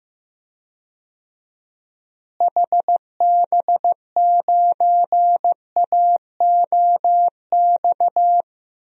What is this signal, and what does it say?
Morse code, heard as a keyed tone, about 15 wpm, 700 Hz. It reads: HB9AOX
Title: DXCC-100 Call Signs - 6 Characters - Repeated 2x - 15wpm